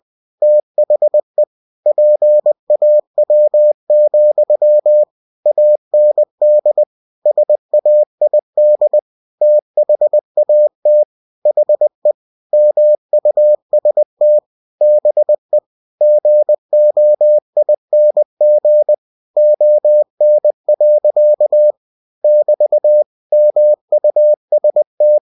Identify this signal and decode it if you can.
Morse code, heard as a keyed tone, about 20 wpm, 600 Hz. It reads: THE PAW, AND SAID THAT HE MUST BE GOING ON. <BT> MUST
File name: cw-books / winnie-the-pooh / winnie-20wpm-page-0015